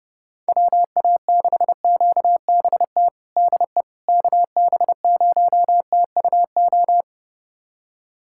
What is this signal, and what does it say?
Morse code, heard as a keyed tone, about 32 wpm, 700 Hz. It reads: WA6QBT DE KB0TUO
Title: USA Callsign DE Callsign 30wpm